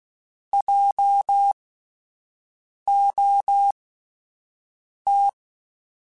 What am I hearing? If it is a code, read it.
Morse code, heard as a keyed tone, about 16 wpm, spaced as for 5 wpm, 800 Hz. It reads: JOT
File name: Spaced Alphas / Spaced Alphas mp3s / Js sp @10